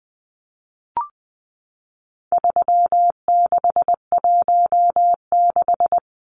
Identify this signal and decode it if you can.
Morse code, heard as a keyed tone, about 20 wpm, 700 Hz. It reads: 3616
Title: Numbers - 4-Digit 20wpm